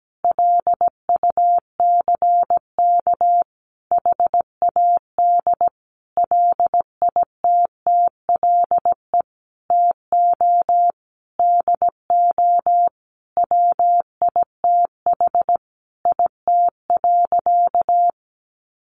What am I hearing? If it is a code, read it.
Morse code, heard as a keyed tone, about 17 wpm, 700 Hz. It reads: LUCK HAD LITTLE TO DO WITH IT.
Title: Sentences from Top 1500 Words 17wpm